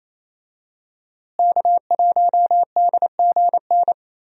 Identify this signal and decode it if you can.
Morse code, heard as a keyed tone, about 28 wpm, 700 Hz. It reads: K1DGN
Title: USA Call Signs - 5 Characters - Repeated 2x - 28wpm